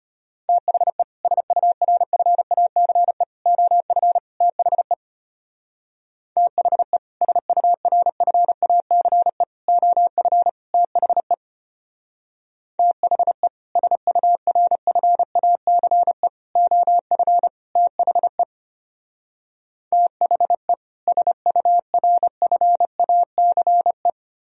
Morse code, heard as a keyed tone, about 33 wpm, 700 Hz. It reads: THE SURFACE OF THE THE SURFACE OF THE THE SURFACE OF THE THE SURFACE